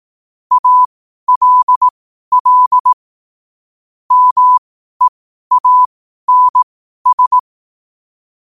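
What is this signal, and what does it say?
Morse code, heard as a keyed tone, about 18 wpm, spaced as for 12 wpm, 1000 Hz. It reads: ALL MEANS
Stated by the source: Morse code sent at 12 words per minute.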